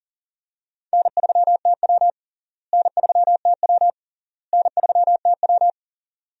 Morse code, heard as a keyed tone, about 40 wpm, 700 Hz. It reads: N3TW N3TW N3TW
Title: USA Call Signs - 4 Characters - Repeated 3x - 40wpm